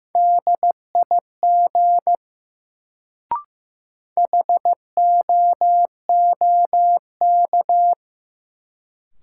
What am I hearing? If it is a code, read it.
Morse code, heard as a keyed tone, about 15 wpm, 700 Hz. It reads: DIG HOOK
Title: Sentences from Top 1600 Words - New Word Drill - 15wpm